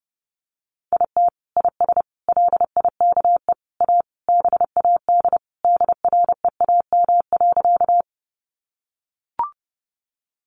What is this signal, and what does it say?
Morse code, heard as a keyed tone, about 30 wpm, 700 Hz. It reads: IT IS LIKE A BAD DREAM.
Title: Sentences from Top 1500 Words 30wpm